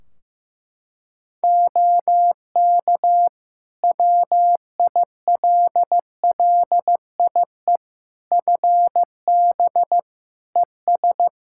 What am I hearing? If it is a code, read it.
Morse code, heard as a keyed tone, about 15 wpm, 700 Hz. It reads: OK WILLIE FB ES
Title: Practice QSO - Exchange 2 - Info - 15wpm